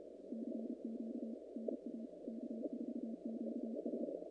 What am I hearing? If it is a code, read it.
Morse code, heard as a keyed tone, about 32 wpm, 220 Hz. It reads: CQ NA K4QS
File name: contests / naqp-cw-k5tr-op-k5pi- -2100